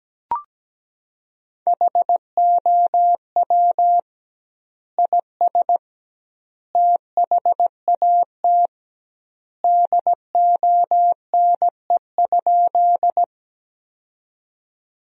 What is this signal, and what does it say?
Morse code, heard as a keyed tone, about 17 wpm, 700 Hz. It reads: HOW IS THAT DONE?
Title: 2x Word-Spacing - Sentences from Top 500 Words - 17wpm